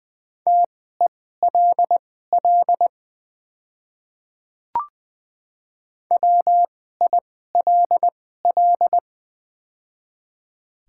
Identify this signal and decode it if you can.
Morse code, heard as a keyed tone, about 20 wpm, spaced as for 14 wpm, 700 Hz. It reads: TELL WILL
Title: 2x Character Spacing - Top 300 Words - 20wpm